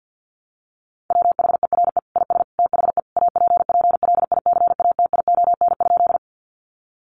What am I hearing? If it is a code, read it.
Morse code, heard as a keyed tone, about 50 wpm, 700 Hz. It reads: WHERE IS THE APPLICATION?
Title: Sentences from Top 1200 Words 50wpm